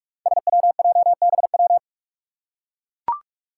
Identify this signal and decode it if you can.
Morse code, heard as a keyed tone, about 44 wpm, 700 Hz. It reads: SWJBW
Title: Five Challenging Letter-Number Combinations 45wpm